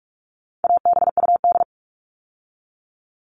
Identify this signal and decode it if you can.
Morse code, heard as a keyed tone, about 44 wpm, 700 Hz. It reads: ABUD
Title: All Cut Numbers - 4-Digit 45wpm